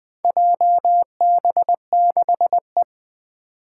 Morse code, heard as a keyed tone, about 20 wpm, 700 Hz. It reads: JB6E
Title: Four Challenging Letter-Number Combinations 20wpm